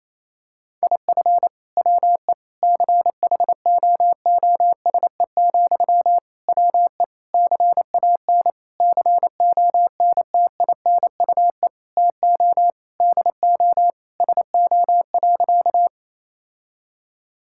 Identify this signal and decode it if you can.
Morse code, heard as a keyed tone, about 28 wpm, 700 Hz. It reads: IF WE CHOOSE, WE CAN CONTINUE TO DO SO.
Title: Sentences from Top 1000 Words - Review Part 6 - 28wpm